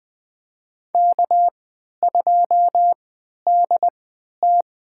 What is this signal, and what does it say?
Morse code, heard as a keyed tone, about 20 wpm, spaced as for 11 wpm, 700 Hz. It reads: K2DT
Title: USA Call Signs - 3x Character Spacing - 20wpm